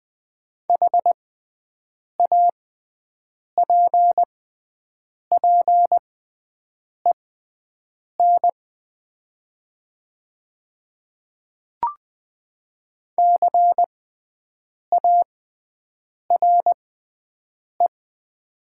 Morse code, heard as a keyed tone, about 20 wpm, spaced as for 7 wpm, 700 Hz. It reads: HAPPEN CARE
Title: 6x Character Spacing - Top 500 Words - 20wpm